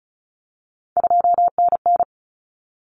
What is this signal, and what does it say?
Morse code, heard as a keyed tone, about 34 wpm, 700 Hz. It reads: 2NN
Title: Common Cut Numbers - 3-Digit 35wpm